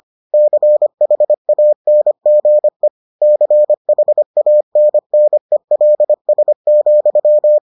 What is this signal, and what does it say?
Morse code, heard as a keyed tone, about 25 wpm, 600 Hz. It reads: CHANGE CHANNELS,